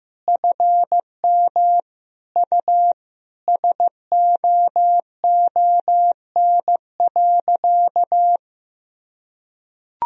Morse code, heard as a keyed tone, about 16 wpm, 700 Hz. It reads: FM U SOON.
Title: Sentences from Top 400 Words and Abbreviations 15wpm